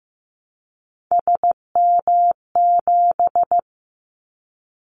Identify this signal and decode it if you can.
Morse code, heard as a keyed tone, about 15 wpm, 700 Hz. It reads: SM7
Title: Callsign Prefixes for Top 25 DXCC - 15wpm